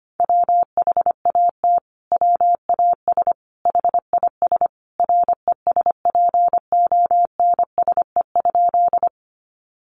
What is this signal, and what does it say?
Morse code, heard as a keyed tone, about 25 wpm, 700 Hz. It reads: WHAT WAS HIS RESPONSE?
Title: Sentences from Top 1400 Words 25wpm